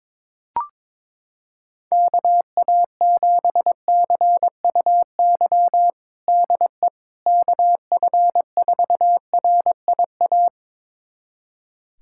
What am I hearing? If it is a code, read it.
Morse code, heard as a keyed tone, about 22 wpm, 700 Hz. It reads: KA7CUY DE KF4RIA
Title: USA Callsign DE Callsign 22wpm